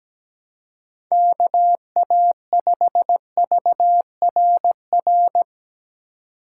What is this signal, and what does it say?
Morse code, heard as a keyed tone, about 17 wpm, 700 Hz. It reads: KA5VRR
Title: USA Call Signs - 6 Characters - 3x Character Spacing - 17wpm